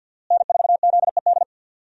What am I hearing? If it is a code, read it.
Morse code, heard as a keyed tone, about 52 wpm, 700 Hz. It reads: N4ZED
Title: USA Call Signs - 5 Characters - Repeated 3x - 50wpm